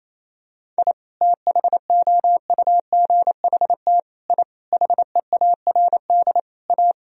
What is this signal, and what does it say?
Morse code, heard as a keyed tone, about 28 wpm, 700 Hz. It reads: I THOUGHT I HEARD A
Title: Sentences from Top 1000 Words - Review Part 1 - 28wpm